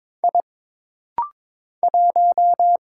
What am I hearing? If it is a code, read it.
Morse code, heard as a keyed tone, about 22 wpm, 700 Hz. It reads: I 1
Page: This rapid-fire timing is geared to push your ICR (Instant Character Recognition) skills.